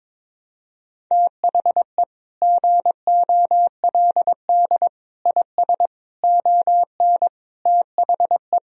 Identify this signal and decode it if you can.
Morse code, heard as a keyed tone, about 22 wpm, 700 Hz. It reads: THE GOLD IS ON THE
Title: Sentences from Top 1000 Words - Review Part 3 - 22wpm